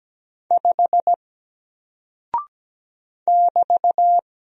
Morse code, heard as a keyed tone, about 17 wpm, 700 Hz. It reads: 5 <BT>
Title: Single Character (Letter:Number:Punctuation:Prosign) Alt 17wpm